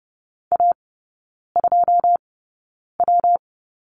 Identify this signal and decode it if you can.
Morse code, heard as a keyed tone, about 30 wpm, spaced as for 9 wpm, 700 Hz. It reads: A2W
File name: Lesson 19e - Letter w - Three Characters 30wpm